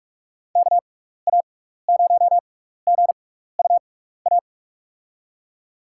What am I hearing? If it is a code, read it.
Morse code, heard as a keyed tone, about 46 wpm, spaced as for 15 wpm, 700 Hz. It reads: KA0GUA
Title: USA Call Signs - 6 Characters - 6x Character Spacing - 45wpm